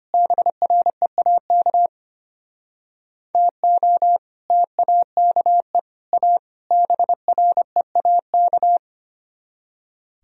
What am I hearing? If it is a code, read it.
Morse code, heard as a keyed tone, about 30 wpm, 700 Hz. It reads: BREAK TO TAKE A BREAK
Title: Speed-Racing - Sets of 4 Words 25wpm